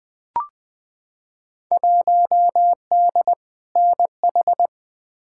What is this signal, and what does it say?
Morse code, heard as a keyed tone, about 20 wpm, 700 Hz. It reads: 1D NH
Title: Field Day Exchanges - 20wpm